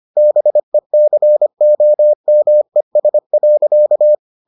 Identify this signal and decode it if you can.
Morse code, heard as a keyed tone, about 25 wpm, 600 Hz. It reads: BECOMES.